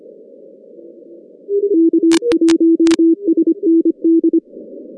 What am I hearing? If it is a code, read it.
Morse code, heard as a keyed tone, about 25 wpm, 330 Hz. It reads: K1SND
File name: 2015 cq ww cw / 2015 cqwww k5tr - 0-18